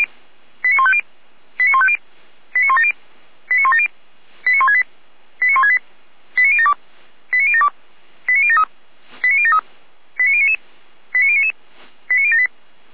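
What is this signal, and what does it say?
Five-tone selective call (ZVEI), heard as quick runs of five tones, tones of about 70 ms. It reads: 89150 89150 89170 89170 89168 89168 89072 89072 89073 89073 89099 89099 89078
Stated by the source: Five-tone ZVEI selective call received by SDR.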